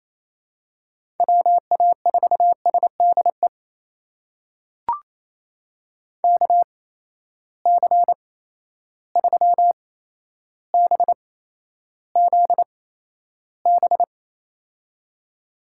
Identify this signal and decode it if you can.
Morse code, heard as a keyed tone, about 27 wpm, spaced as for 12 wpm, 700 Hz. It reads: WA4SDE KC3BZB
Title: USA Call Signs - 6 Characters - 8x Character Spacing - 28wpm